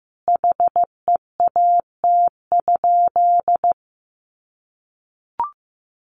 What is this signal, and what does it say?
Morse code, heard as a keyed tone, about 15 wpm, 700 Hz. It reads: HEAT?
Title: Sentences from Top 1400 Words 15wpm